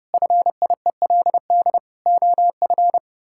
Morse code, heard as a keyed tone, about 32 wpm, 700 Hz. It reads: FIELD OF
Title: Pangram Challenge v2 - 30wpm